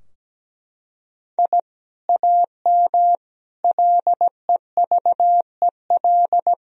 Morse code, heard as a keyed tone, about 17 wpm, 700 Hz. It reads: I AM LEVEL
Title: Sentences from Top 1000 Words - Review Part 4 - 17wpm